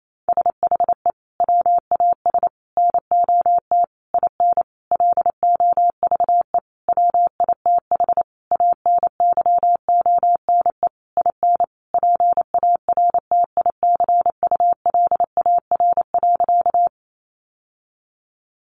Morse code, heard as a keyed tone, about 28 wpm, 700 Hz. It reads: SHE WAS NOT IN LOVE WITH ANYONE IN PARTICULAR.
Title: Sentences from Top 900 Words 28wpm